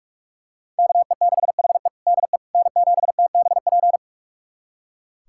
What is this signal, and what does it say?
Morse code, heard as a keyed tone, about 42 wpm, 700 Hz. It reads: KE6HE DE N7TBP
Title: USA Callsign DE Callsign - 5 Characters - 45wpm